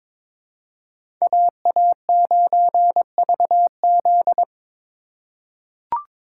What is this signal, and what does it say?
Morse code, heard as a keyed tone, about 21 wpm, 700 Hz. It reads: AA9VZ
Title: USA Call Signs - 5 Characters - 4x Character Spacing - 22wpm